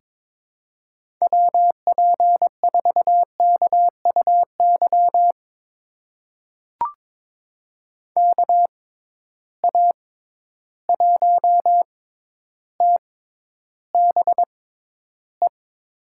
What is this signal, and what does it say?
Morse code, heard as a keyed tone, about 22 wpm, spaced as for 11 wpm, 700 Hz. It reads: WP4KUY KA1TBE
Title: USA Call Signs - 6 Characters - 6x Character Spacing - 22wpm